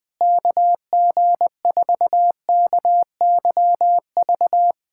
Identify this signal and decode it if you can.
Morse code, heard as a keyed tone, about 20 wpm, 700 Hz. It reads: KG4KYV
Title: USA Callsign DE Callsign - 6 Characters - 20wpm